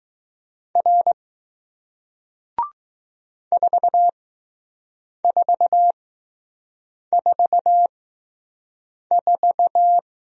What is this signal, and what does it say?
Morse code, heard as a keyed tone, about 20 wpm, spaced as for 6 wpm, 700 Hz. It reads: R 4444